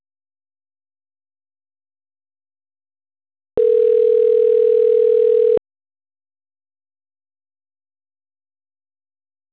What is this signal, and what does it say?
Ringback tone (North America), 440 + 480 Hz.
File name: autotestagent / 10-59-02.136..callcentertransfer. .e096a5ec-4457-4c5e-afa3